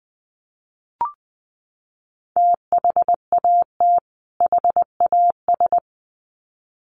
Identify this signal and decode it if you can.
Morse code, heard as a keyed tone, about 20 wpm, 700 Hz. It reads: THAT HAS
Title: Sets of 2 Words 20wpm